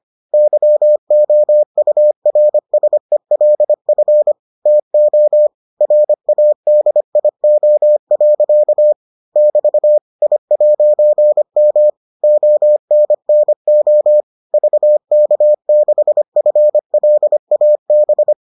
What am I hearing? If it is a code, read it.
Morse code, heard as a keyed tone, about 25 wpm, 600 Hz. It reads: YOURSELF TO RADIO. <BT> I'M ONNO VK6FLAB